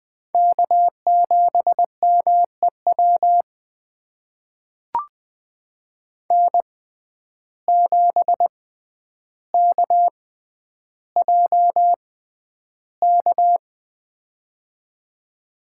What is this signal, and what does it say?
Morse code, heard as a keyed tone, about 20 wpm, spaced as for 10 wpm, 700 Hz. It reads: K7MEW N7KJK
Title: USA Call Signs - 6x Character Spacing - 20wpm